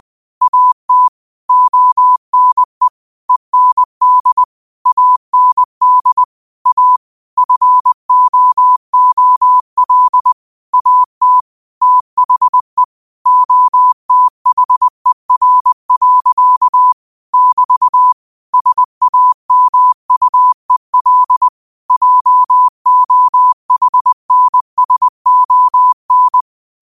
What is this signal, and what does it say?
Morse code, heard as a keyed tone, about 20 wpm, 1000 Hz. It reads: AT ONE END AND A FOOL AT THE OTHER. <BT> SAMUEL JOHNSON
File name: qotdfiles / 20wpm